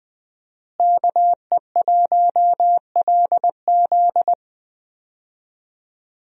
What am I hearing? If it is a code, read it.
Morse code, heard as a keyed tone, about 20 wpm, 700 Hz. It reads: KE1LZ
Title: USA Call Signs - 5 Characters - Repeated 2x - 20wpm